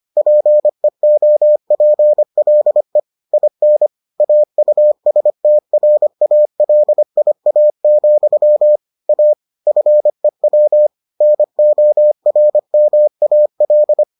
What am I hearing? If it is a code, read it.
Morse code, heard as a keyed tone, about 25 wpm, 600 Hz. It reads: PEOPLE IN AUSTRALIA, A FEW NORMAL